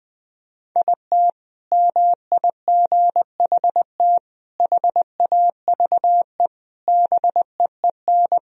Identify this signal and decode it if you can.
Morse code, heard as a keyed tone, about 20 wpm, 700 Hz. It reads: IT MIGHT HAVE BEEN
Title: Sentences from Top 1000 Words - Review Part 3 - 20wpm